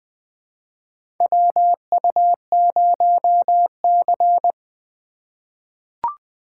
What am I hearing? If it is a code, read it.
Morse code, heard as a keyed tone, about 20 wpm, 700 Hz. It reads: WU0C
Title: USA Call Signs - 4 Characters - 4x Character Spacing - 20wpm